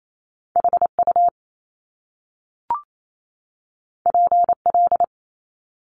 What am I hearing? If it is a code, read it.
Morse code, heard as a keyed tone, about 28 wpm, 700 Hz. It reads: HU PL
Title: Top Two Letter Combinations in Words 28wpm